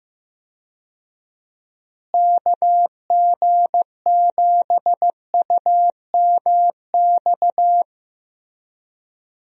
Morse code, heard as a keyed tone, about 15 wpm, 700 Hz. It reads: KG7UMX
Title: USA Call Signs - 6 Characters - Repeated 3x - 15wpm